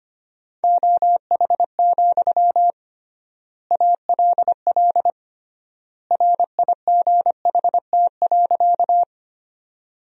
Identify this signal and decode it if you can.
Morse code, heard as a keyed tone, about 25 wpm, 700 Hz. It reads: OH, ALL RIGHT.
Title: 3x Word-Spacing - Sentences from Top 500 Words - 25wpm